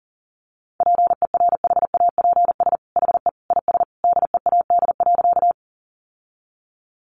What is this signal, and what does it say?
Morse code, heard as a keyed tone, about 40 wpm, 700 Hz. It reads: PERHAPS HE IS DEAD.
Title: Sentences from Top 1500 Words 40wpm